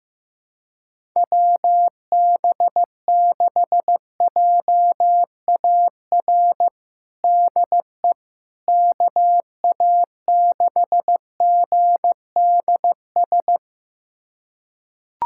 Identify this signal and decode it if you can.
Morse code, heard as a keyed tone, about 15 wpm, 700 Hz. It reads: WB6JAR DE KA6GDS
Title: USA Callsign DE Callsign - 6 Characters - 15wpm